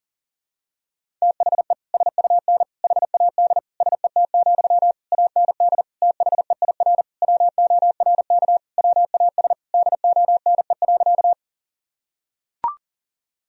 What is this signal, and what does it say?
Morse code, heard as a keyed tone, about 40 wpm, 700 Hz. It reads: THE SUN HAD SET, AND THEIR WORK WAS DONE.
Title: Sentences from Top 1000 Words - Review Part 1 - 40wpm